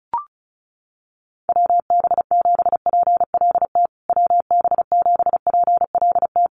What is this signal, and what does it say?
Morse code, heard as a keyed tone, about 34 wpm, 700 Hz. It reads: WB7PLT WB7PLT
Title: USA Call Signs Repeated 3x 35wpm